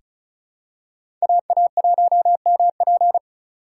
Morse code, heard as a keyed tone, about 35 wpm, 700 Hz. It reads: AA1MP
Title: USA Call Signs - 8x Character Spacing - 35wpm